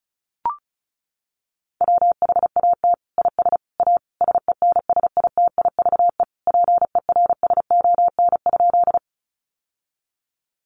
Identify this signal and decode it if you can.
Morse code, heard as a keyed tone, about 34 wpm, 700 Hz. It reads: WHAT IS A SENSITIVE PERSON?
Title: Sentences from Top 1400 Words 35wpm